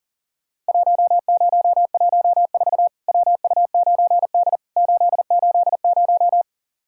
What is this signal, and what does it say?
Morse code, heard as a keyed tone, about 40 wpm, 700 Hz. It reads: 1014 WU9D 880
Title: Worked All Europe DX Contest - QTC Exchange - 40wpm